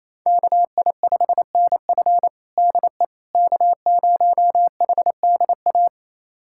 Morse code, heard as a keyed tone, about 28 wpm, 700 Hz. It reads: KI5NF DE K0HDA
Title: USA Callsign DE Callsign - 5 Characters - 28wpm